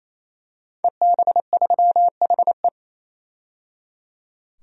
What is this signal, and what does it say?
Morse code, heard as a keyed tone, about 28 wpm, 700 Hz. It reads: EB3HE
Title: DXCC-100 Call Signs - 5 Characters - 28wpm